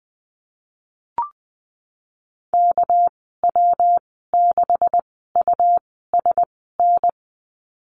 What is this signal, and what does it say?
Morse code, heard as a keyed tone, about 19 wpm, spaced as for 14 wpm, 700 Hz. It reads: KW6USN
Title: 207 Lesson 33g - Letter j - Call Signs 20wpm